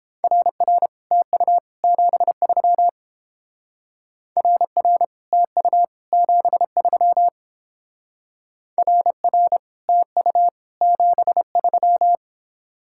Morse code, heard as a keyed tone, about 30 wpm, 700 Hz. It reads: RR TU 73 RR TU 73 RR TU 73